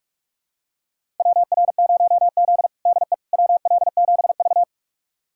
Morse code, heard as a keyed tone, about 41 wpm, 700 Hz. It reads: WR0Z DE WL7V
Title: USA Callsign DE Callsign - 4 Characters - 45wpm